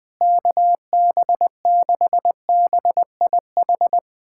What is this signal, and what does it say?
Morse code, heard as a keyed tone, about 21 wpm, 700 Hz. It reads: KB6BIH
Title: USA Callsign DE Callsign - 6 Characters - 20wpm